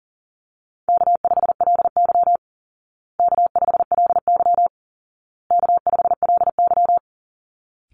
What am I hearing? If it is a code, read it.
Morse code, heard as a keyed tone, about 40 wpm, 700 Hz. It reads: K5LY K5LY K5LY
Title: Call Signs Repeated 2x - 40wpm